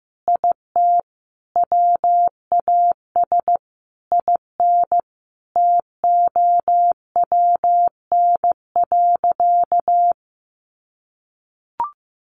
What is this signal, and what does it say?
Morse code, heard as a keyed tone, about 14 wpm, 700 Hz. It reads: IT WAS IN TOWN.
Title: Sentences from Top 1500 Words 15wpm